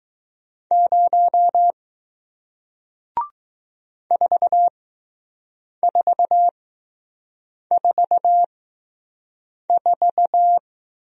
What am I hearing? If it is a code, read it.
Morse code, heard as a keyed tone, about 20 wpm, spaced as for 6 wpm, 700 Hz. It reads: 0 4444